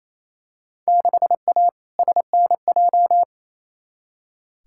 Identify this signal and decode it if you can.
Morse code, heard as a keyed tone, about 28 wpm, 700 Hz. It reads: 6A SNJ
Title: Field Day Exchanges Encore - 28wpm